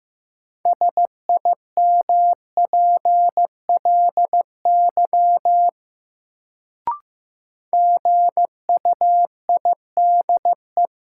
Morse code, heard as a keyed tone, about 15 wpm, 700 Hz. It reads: SIMPLY GUIDE
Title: Sentences from Top 600 Words - New Word Drill - 15wpm